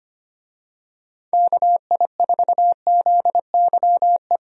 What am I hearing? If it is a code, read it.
Morse code, heard as a keyed tone, about 25 wpm, 700 Hz. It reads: KI4ZYE
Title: USA Call Signs - 6 Characters - 25wpm